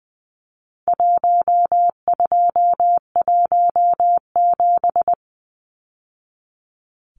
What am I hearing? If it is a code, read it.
Morse code, heard as a keyed tone, about 20 wpm, 700 Hz. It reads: 1217
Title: Numbers - Contest Style - 4-Digit 20wpm